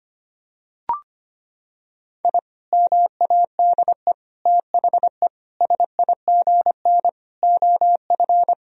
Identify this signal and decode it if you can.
Morse code, heard as a keyed tone, about 25 wpm, 700 Hz. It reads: I MADE THE SIGN OF
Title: Sentences from Top 1000 Words - Review Part 1 - 25wpm